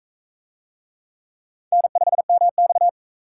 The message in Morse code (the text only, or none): N5MX